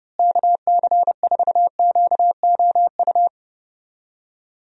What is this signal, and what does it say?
Morse code, heard as a keyed tone, about 30 wpm, 700 Hz. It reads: KC4QOU
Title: USA Call Signs - 6 Characters - Repeated 2x - 30wpm